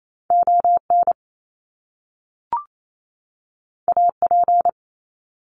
Morse code, heard as a keyed tone, about 28 wpm, 700 Hz. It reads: ON AP
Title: Top Two Letter Combinations in Words 28wpm